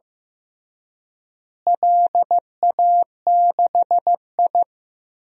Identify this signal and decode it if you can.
Morse code, heard as a keyed tone, about 15 wpm, 700 Hz. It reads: LA6I
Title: DXCC-100 Call Signs - 4 Characters - Repeated 3x - 15wpm